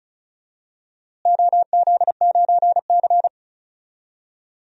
Morse code, heard as a keyed tone, about 36 wpm, 700 Hz. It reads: OZ9C
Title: DXCC-100 Call Signs - 4 Characters - Repeated 2x - 35wpm